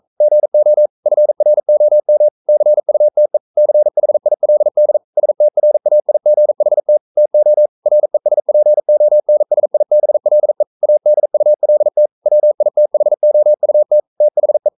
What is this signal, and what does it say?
Morse code, heard as a keyed tone, about 42 wpm, 600 Hz. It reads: GO FROM CUTE CHILD STRAIGHT TO RESPONSIBLE ADULT WITHOUT THE
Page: Romanen Never af Ken Follet på engelsk.